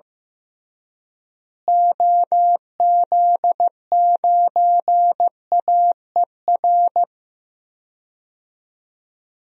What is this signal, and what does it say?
Morse code, heard as a keyed tone, about 15 wpm, 700 Hz. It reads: OZ9AER
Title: DXCC-100 Call Signs - 6 Characters - Repeated 2x - 15wpm